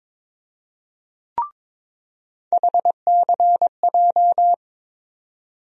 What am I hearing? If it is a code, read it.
Morse code, heard as a keyed tone, about 22 wpm, 700 Hz. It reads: HCJ
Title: USA Call Sign Postfixes - 3 Letters 22wpm